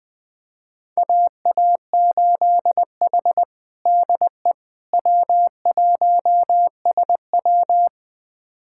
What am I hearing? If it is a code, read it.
Morse code, heard as a keyed tone, about 20 wpm, 700 Hz. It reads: AA8H DE W1SW
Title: USA Callsign DE Callsign - 4 Characters - 20wpm